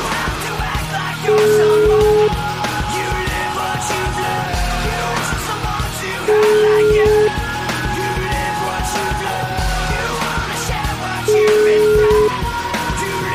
Ringback tone (Europe, 425 Hz), with noise.